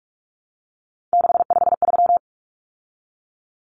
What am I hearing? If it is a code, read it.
Morse code, heard as a keyed tone, about 44 wpm, 700 Hz. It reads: 653
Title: Numbers - 3-Digit 45wpm